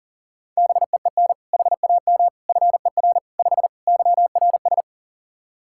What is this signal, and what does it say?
Morse code, heard as a keyed tone, about 40 wpm, 700 Hz. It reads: BEEN HAM FER 5 YRS
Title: Practice QSO - Exchange 3 - More Info - 40wpm